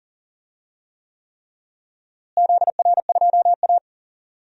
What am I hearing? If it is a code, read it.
Morse code, heard as a keyed tone, about 40 wpm, 700 Hz. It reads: ZR2A
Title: DXCC-100 Call Signs - 4 Characters - Repeated 3x - 40wpm